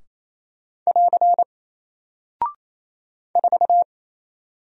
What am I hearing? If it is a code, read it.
Morse code, heard as a keyed tone, about 27 wpm, 700 Hz. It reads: <AR> 4
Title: Single Character (Letter:Number:Punctuation:Prosign) 28wpm